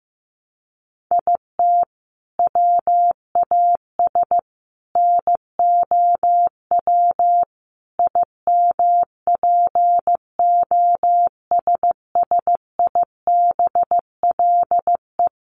Morse code, heard as a keyed tone, about 15 wpm, 700 Hz. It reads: IT WAS NOW IMPOSSIBLE
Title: Sentences from Top 1100 Words 15wpm